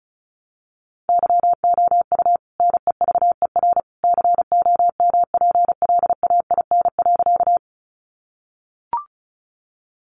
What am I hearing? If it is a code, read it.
Morse code, heard as a keyed tone, about 34 wpm, 700 Hz. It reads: YOU NEVER COMPLAIN.
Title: Sentences from Top 1300 Words 35wpm